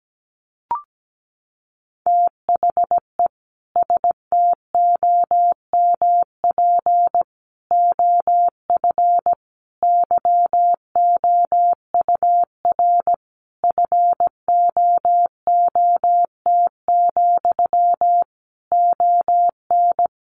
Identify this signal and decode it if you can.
Morse code, heard as a keyed tone, about 17 wpm, 700 Hz. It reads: THE STOMP OF YOUR FOOT, ON
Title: A Sound Of Thunder 17wpm